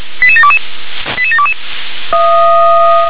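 Five-tone selective call (ZVEI), heard as quick runs of five tones, tones of about 70 ms. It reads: 88722 88722
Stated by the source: ZVEI five-tone selective call received on an SDR.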